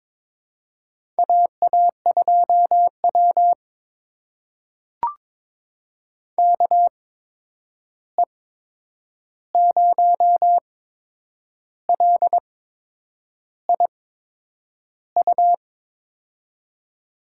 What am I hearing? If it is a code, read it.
Morse code, heard as a keyed tone, about 22 wpm, spaced as for 8 wpm, 700 Hz. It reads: AA2W KE0LIU
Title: USA Call Signs - 8x Character Spacing - 22wpm